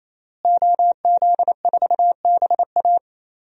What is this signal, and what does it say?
Morse code, heard as a keyed tone, about 27 wpm, 700 Hz. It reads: OZ4BA
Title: DXCC-100 Call Signs - 5 Characters - Repeated 3x - 28wpm